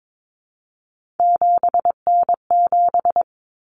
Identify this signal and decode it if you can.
Morse code, heard as a keyed tone, about 22 wpm, 700 Hz. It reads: 7N7
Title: Common Cut Numbers - 3-Digit 22wpm